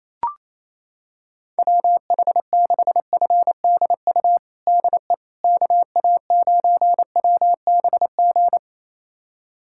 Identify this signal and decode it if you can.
Morse code, heard as a keyed tone, about 28 wpm, 700 Hz. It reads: WH6FDU DE KA9WBG
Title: USA Callsign DE Callsign - 6 Characters - 28wpm